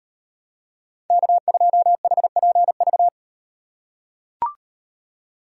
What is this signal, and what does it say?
Morse code, heard as a keyed tone, about 35 wpm, 700 Hz. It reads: K2HPV